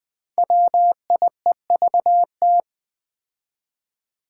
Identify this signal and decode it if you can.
Morse code, heard as a keyed tone, about 20 wpm, 700 Hz. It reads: WIEVT
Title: Five Challenging Letter-Number Combinations 20wpm